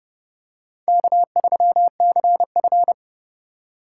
Morse code, heard as a keyed tone, about 32 wpm, 700 Hz. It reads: K3CF
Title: USA Call Signs - 4 Characters - 2x Character Spacing - 30wpm